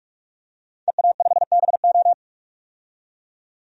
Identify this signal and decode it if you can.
Morse code, heard as a keyed tone, about 41 wpm, 700 Hz. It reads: EA5BO
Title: DXCC-100 Call Signs - 5 Characters - 45wpm